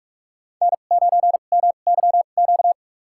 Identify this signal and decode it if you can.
Morse code, heard as a keyed tone, about 44 wpm, spaced as for 32 wpm, 700 Hz. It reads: N9MYQ
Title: USA Call Signs - 5 Characters - 2x Character Spacing - 45wpm